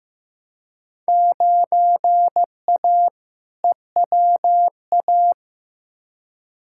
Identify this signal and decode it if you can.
Morse code, heard as a keyed tone, about 15 wpm, 700 Hz. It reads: 9A EWA
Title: Field Day Exchanges Encore - 15wpm